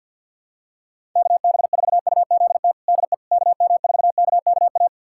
Morse code, heard as a keyed tone, about 52 wpm, 700 Hz. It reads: KB4UZT DE KM4KKA
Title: USA Callsign DE Callsign - 6 Characters - 50wpm